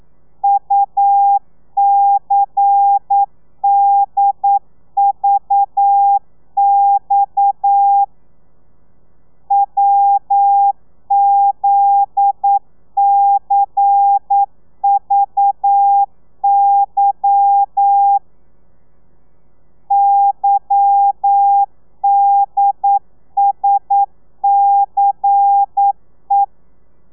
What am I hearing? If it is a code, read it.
Morse code, heard as a keyed tone, about 9 wpm, 800 Hz. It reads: UCDVX WZCVY YDSCE